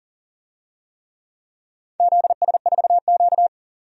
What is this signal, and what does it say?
Morse code, heard as a keyed tone, about 40 wpm, 690 Hz. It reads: ZS4Q
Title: DXCC-100 Call Signs - 4 Characters - 40wpm